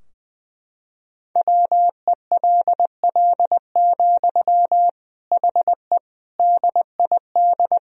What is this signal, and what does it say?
Morse code, heard as a keyed tone, about 20 wpm, 700 Hz. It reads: WELL, HE DID
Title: Sentences from Top 1000 Words - Review Part 2 - 20wpm